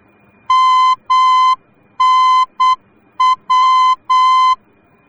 Morse code, heard as a keyed tone, about 8 wpm, 1030 Hz. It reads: MNW
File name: germany / mx / MNW-D